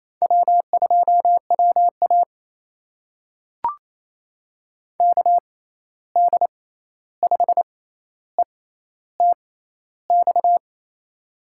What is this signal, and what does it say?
Morse code, heard as a keyed tone, about 28 wpm, spaced as for 13 wpm, 700 Hz. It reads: W2WA KD5ETX
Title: USA Call Signs - 6x Character Spacing - 28wpm